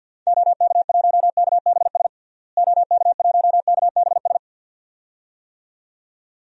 Morse code, heard as a keyed tone, about 47 wpm, 700 Hz. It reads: OK1KBS OK1KBS
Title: DXCC-100 Call Signs - 6 Characters - Repeated 2x - 50wpm